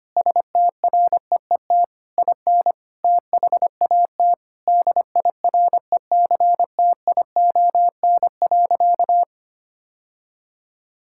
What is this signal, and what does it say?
Morse code, heard as a keyed tone, about 25 wpm, 700 Hz. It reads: STREET IN THAT DIRECTION.
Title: Sentences from Top 1000 Words - Review Part 5 - 25wpm